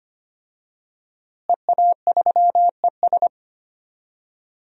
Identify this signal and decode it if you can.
Morse code, heard as a keyed tone, about 25 wpm, 700 Hz. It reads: EA3ES
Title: DXCC-100 Call Signs - 5 Characters - Repeated 2x - 25wpm